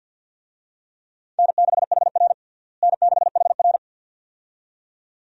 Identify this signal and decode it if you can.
Morse code, heard as a keyed tone, about 47 wpm, 700 Hz. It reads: N6HR N6HR
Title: USA Call Signs - 4 Characters - Repeated 2x - 50wpm